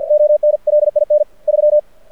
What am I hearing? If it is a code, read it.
Morse code, heard as a keyed tone, about 51 wpm, 600 Hz. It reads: ONLIN 3
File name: archived / CRSnet